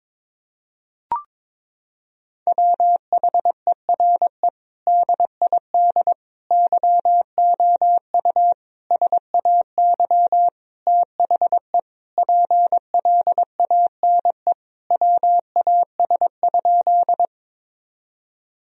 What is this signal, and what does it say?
Morse code, heard as a keyed tone, about 21 wpm, 700 Hz. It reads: WHERE DID YOU SAY THE PLANE WAS?
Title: Sentences from Top 1000 Words - Review Part 3 - 22wpm